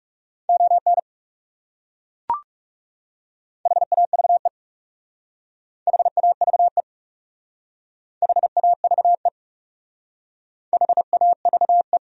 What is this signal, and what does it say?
Morse code, heard as a keyed tone, about 36 wpm, 700 Hz. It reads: ON HAVE HAVE HAVE HAVE
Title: Speed-Racing Top 200 Words 30wpm